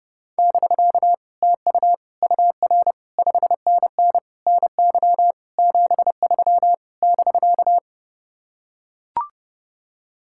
Morse code, heard as a keyed tone, about 30 wpm, 700 Hz. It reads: <BK> TU UR 5NN NY 73 <BK>